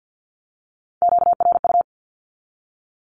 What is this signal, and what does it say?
Morse code, heard as a keyed tone, about 48 wpm, 700 Hz. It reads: QRU
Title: Q-Codes 50wpm